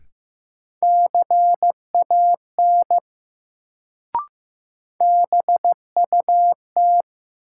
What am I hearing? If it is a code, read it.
Morse code, heard as a keyed tone, about 15 wpm, 700 Hz. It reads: CAN BUT
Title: Sentences from Top 100 Words - New Word Drill - 15wpm